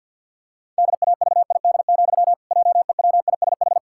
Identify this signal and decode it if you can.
Morse code, heard as a keyed tone, about 50 wpm, 700 Hz. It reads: DAVID, JEWISH
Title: Pangram Challenge v2 - 50wpm